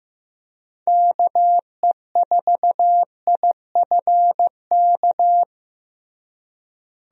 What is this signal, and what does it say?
Morse code, heard as a keyed tone, about 15 wpm, 700 Hz. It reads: KE4IFK
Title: USA Call Signs - 6 Characters - Repeated 3x - 15wpm